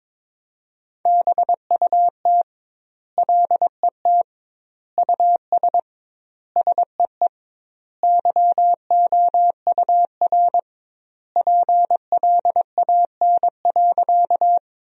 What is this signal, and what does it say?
Morse code, heard as a keyed tone, about 22 wpm, 700 Hz. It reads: BUT LET US SEE YOUR PLAN.
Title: 2x Word-Spacing - Sentences from Top 500 Words - 22wpm